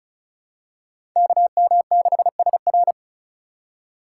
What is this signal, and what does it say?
Morse code, heard as a keyed tone, about 36 wpm, 700 Hz. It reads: KM6SR
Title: USA Call Signs - 5 Characters - 4x Character Spacing - 35wpm